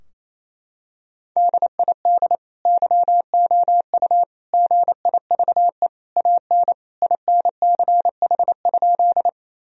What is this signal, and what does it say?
Morse code, heard as a keyed tone, about 28 wpm, 700 Hz. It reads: DID YOU GIVE AN INCH?
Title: Sentences from Top 1000 Words - Review Part 3 - 28wpm